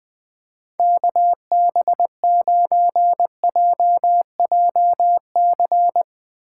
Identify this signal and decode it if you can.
Morse code, heard as a keyed tone, about 20 wpm, 700 Hz. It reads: KB9JJC
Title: USA Call Signs - 4x Character Spacing - 20wpm